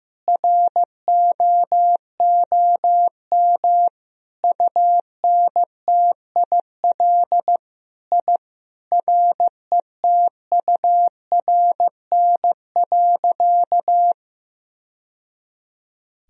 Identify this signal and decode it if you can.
Morse code, heard as a keyed tone, about 15 wpm, 700 Hz. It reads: ROOM UNTIL I RETURN.
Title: Sentences from Top 1000 Words - Review Part 1 - 15wpm